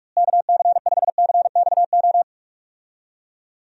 Morse code, heard as a keyed tone, about 44 wpm, 700 Hz. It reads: KC5CXO
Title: USA Call Signs - 6 Characters - Repeated 2x - 45wpm